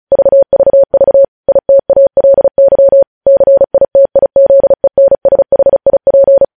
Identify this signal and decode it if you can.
Morse code, heard as a keyed tone, about 36 wpm, 550 Hz. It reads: VVV ITALY CITIZENSHIP
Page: news headlines in morse code at 35 wpm